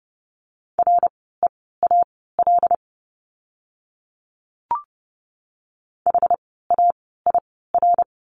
Morse code, heard as a keyed tone, about 30 wpm, spaced as for 17 wpm, 700 Hz. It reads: REAL HAIR
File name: Lesson 17f - Letter u - Words 30wpm